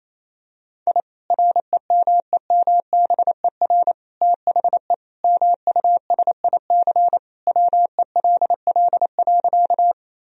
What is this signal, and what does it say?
Morse code, heard as a keyed tone, about 28 wpm, 700 Hz. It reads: I REMEMBER THE MUSIC WELL.
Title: Sentences from Top 1000 Words - Review Part 1 - 28wpm